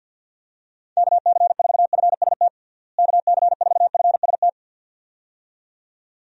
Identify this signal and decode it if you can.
Morse code, heard as a keyed tone, about 51 wpm, 700 Hz. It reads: KC4FST KC4FST
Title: USA Call Signs - 6 Characters - Repeated 2x - 50wpm